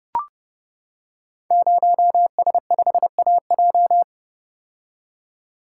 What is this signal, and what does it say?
Morse code, heard as a keyed tone, about 30 wpm, 700 Hz. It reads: 0S5AJ
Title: Code Groups - 5 Letter:Numbers - 30wpm